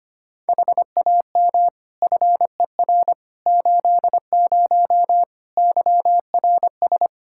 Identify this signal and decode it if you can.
Morse code, heard as a keyed tone, about 25 wpm, 700 Hz. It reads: HAM FER 80 YRS
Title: Practice QSO - Exchange 3 - More Info - 25wpm